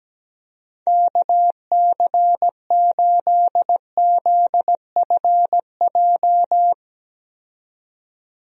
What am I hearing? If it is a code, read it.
Morse code, heard as a keyed tone, about 17 wpm, 700 Hz. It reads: KC8ZFJ
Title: USA Call Signs - 6 Characters - 17wpm